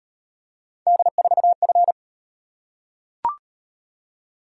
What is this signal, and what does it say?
Morse code, heard as a keyed tone, about 31 wpm, 700 Hz. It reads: D4F